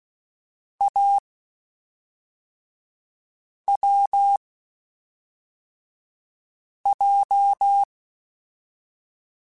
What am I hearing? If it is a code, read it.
Morse code, heard as a keyed tone, about 16 wpm, spaced as for 3 wpm, 800 Hz. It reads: AWJ